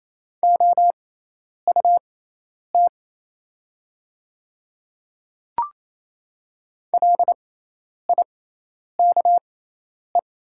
Morse code, heard as a keyed tone, about 28 wpm, spaced as for 10 wpm, 700 Hz. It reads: OUT LIKE
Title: 6x Character Spacing - Top 300 Words - 28wpm